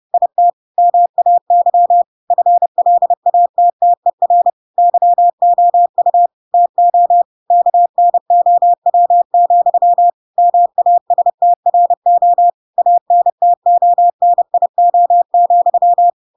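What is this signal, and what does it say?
Morse code, heard as a keyed tone, about 30 wpm, 700 Hz. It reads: IT MAY FLATTER YOU TO KNOW, MASTRO ANTONIO,
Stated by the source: Chapter 2 of The Adventures of Pinocchio read as a book in Morse code (CW) at 30 wpm, by ik1hge.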